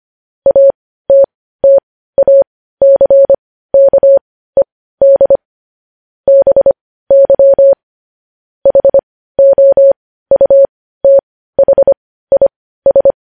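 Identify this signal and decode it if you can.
Morse code, heard as a keyed tone, about 25 wpm, spaced as for 15 wpm, 550 Hz. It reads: ATTACKED BY HOUTHIS
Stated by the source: Morse news headlines practice sent at 15 WPM.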